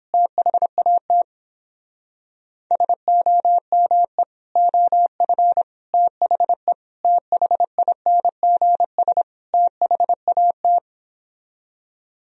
Morse code, heard as a keyed tone, about 29 wpm, 700 Hz. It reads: THAT SOME OF THE THINGS THAT